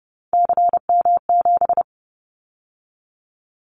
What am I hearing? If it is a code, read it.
Morse code, heard as a keyed tone, about 30 wpm, 700 Hz. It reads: CM7
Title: Callsign Prefixes for Top 100 DXCC - 30wpm